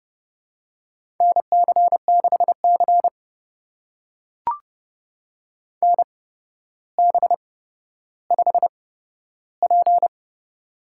Morse code, heard as a keyed tone, about 30 wpm, spaced as for 13 wpm, 700 Hz. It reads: NC6C NB5P
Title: USA Call Signs - 4 Characters - 8x Character Spacing - 30wpm